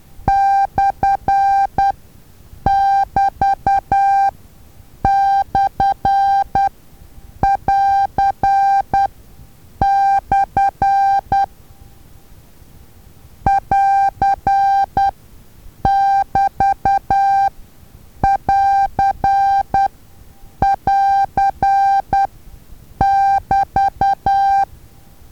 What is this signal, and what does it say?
Morse code, heard as a keyed tone, about 10 wpm, spaced as for 7 wpm, 790 Hz. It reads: /<BT>/<AR>/ <AR><BT><AR><AR><BT>
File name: Taviro HA6VA A